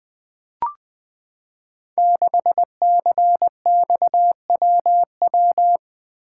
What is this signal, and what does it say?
Morse code, heard as a keyed tone, about 20 wpm, 700 Hz. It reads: 6CXWW
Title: Code Groups - 5 Letter:Numbers - 20wpm